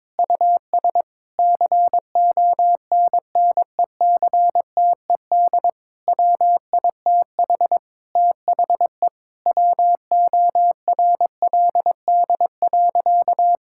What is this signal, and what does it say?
Morse code, heard as a keyed tone, about 22 wpm, 700 Hz. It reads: US CONNECTED WITH THE WORLD.
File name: Difficult Sentences with Hints - 22wpm